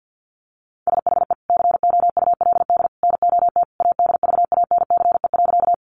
Morse code, heard as a keyed tone, about 50 wpm, 700 Hz. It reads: SHE COULD NOT ADVANCE.
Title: Sentences from Top 1400 Words 50wpm